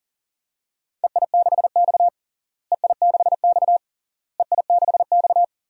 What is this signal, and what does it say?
Morse code, heard as a keyed tone, about 40 wpm, 700 Hz. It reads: EI6X EI6X EI6X
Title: DXCC-100 Call Signs - 4 Characters - Repeated 3x - 40wpm